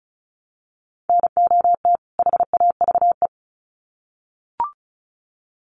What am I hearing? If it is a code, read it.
Morse code, heard as a keyed tone, about 34 wpm, 700 Hz. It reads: NOT HAVE
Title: Sets of 2 Words 35wpm